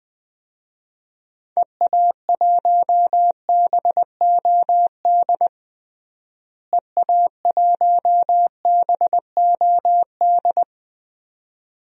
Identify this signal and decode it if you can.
Morse code, heard as a keyed tone, about 20 wpm, 700 Hz. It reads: EA1BOD EA1BOD
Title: DXCC-100 Call Signs - 6 Characters - Repeated 2x - 20wpm